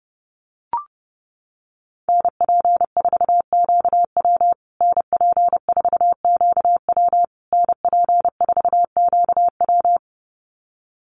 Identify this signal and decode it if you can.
Morse code, heard as a keyed tone, about 30 wpm, 700 Hz. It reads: NP4QW NP4QW NP4QW
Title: USA Call Signs Repeated 3x 30wpm